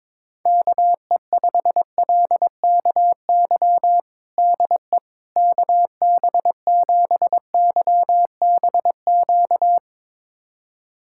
Morse code, heard as a keyed tone, about 22 wpm, 700 Hz. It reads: KE5LKY DE KB7YBQ
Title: USA Callsign DE Callsign - 6 Characters - 22wpm